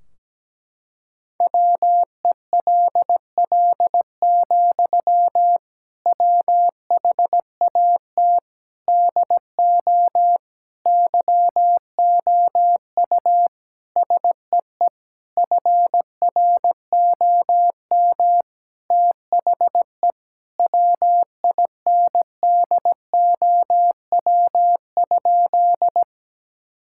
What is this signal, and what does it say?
Morse code, heard as a keyed tone, about 17 wpm, 700 Hz. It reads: WELL, WHAT DO YOU SEE FROM THE WINDOW?
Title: Sentences from Top 1600 Words 17wpm